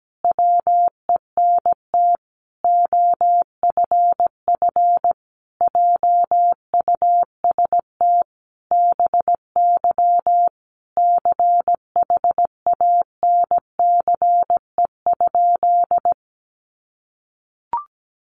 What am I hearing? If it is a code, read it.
Morse code, heard as a keyed tone, about 17 wpm, 700 Hz. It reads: WENT OFF JUST BY CHANCE?
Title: Sentences from Top 800 Words 17wpm